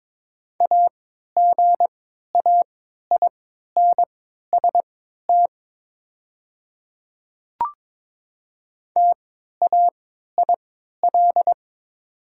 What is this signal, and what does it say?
Morse code, heard as a keyed tone, about 22 wpm, spaced as for 13 wpm, 700 Hz. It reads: AGAINST TAIL
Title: 3x Character Spacing - Top 500 Words - 22wpm